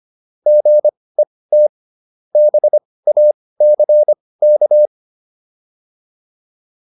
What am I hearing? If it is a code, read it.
Morse code, heard as a keyed tone, about 25 wpm, spaced as for 18 wpm, 600 Hz. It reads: GET BACK